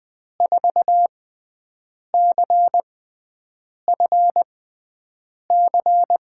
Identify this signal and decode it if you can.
Morse code, heard as a keyed tone, about 20 wpm, spaced as for 7 wpm, 700 Hz. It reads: 4CFC
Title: USA Call Signs - 6 Characters - 6x Character Spacing - 20wpm